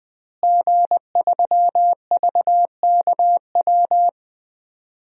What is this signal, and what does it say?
Morse code, heard as a keyed tone, about 20 wpm, 700 Hz. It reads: G3VKW
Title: DXCC-100 Call Signs - 5 Characters - Repeated 2x - 20wpm